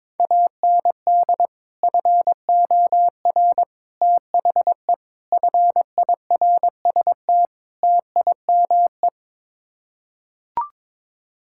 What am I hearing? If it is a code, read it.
Morse code, heard as a keyed tone, about 22 wpm, 700 Hz. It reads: AND FOR THE FIRST TIME
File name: practice / 22wpm / Build A Phrase - 5-Word Phrases - 22wpm